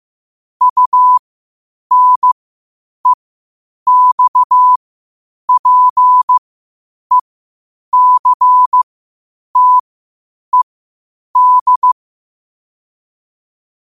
Morse code, heard as a keyed tone, about 15 wpm, spaced as for 8 wpm, 1000 Hz. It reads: UNEXPECTED
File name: qotdfiles / 8wpm